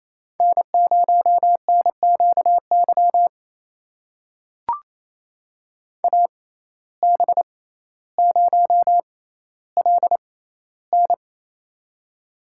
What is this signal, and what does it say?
Morse code, heard as a keyed tone, about 28 wpm, spaced as for 14 wpm, 700 Hz. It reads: N0NQY AB0LN
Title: USA Call Signs - 5 Characters - 6x Character Spacing - 28wpm